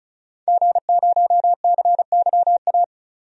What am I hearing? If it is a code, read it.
Morse code, heard as a keyed tone, about 36 wpm, 700 Hz. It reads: G0CYA
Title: DXCC-100 Call Signs - 5 Characters - 35wpm